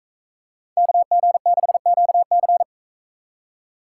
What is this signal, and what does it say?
Morse code, heard as a keyed tone, about 43 wpm, 700 Hz. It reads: KG6QC